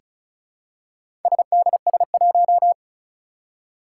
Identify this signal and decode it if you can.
Morse code, heard as a keyed tone, about 36 wpm, 700 Hz. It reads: SDS1
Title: Four Challenging Letter-Number Combinations 35wpm